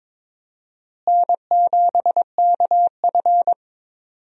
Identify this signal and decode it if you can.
Morse code, heard as a keyed tone, about 22 wpm, 700 Hz. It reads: N7KF
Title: USA Call Signs - 4 Characters - Repeated 3x - 22wpm